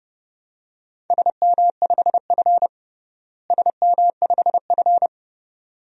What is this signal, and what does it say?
Morse code, heard as a keyed tone, about 30 wpm, 700 Hz. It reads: SM5F SM5F
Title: DXCC-100 Call Signs - 4 Characters - Repeated 2x - 30wpm